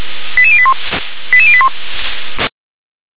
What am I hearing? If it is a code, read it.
Five-tone selective call (ZVEI), heard as quick runs of five tones, tones of about 70 ms. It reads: 88081 88081